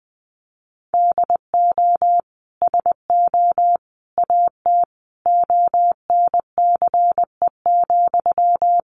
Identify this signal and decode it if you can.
Morse code, heard as a keyed tone, about 20 wpm, 700 Hz. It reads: DO SO AT ONCE,
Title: Sentences from Top 1500 Words 20wpm